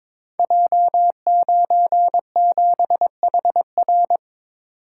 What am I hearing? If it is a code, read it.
Morse code, heard as a keyed tone, about 22 wpm, 700 Hz. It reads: J97HR
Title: Code Groups - 5 Letter:Numbers - 22wpm